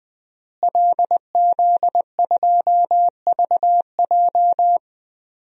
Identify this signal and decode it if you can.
Morse code, heard as a keyed tone, about 20 wpm, 700 Hz. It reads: LZ2VJ
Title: DXCC-100 Call Signs - 5 Characters - Repeated 2x - 20wpm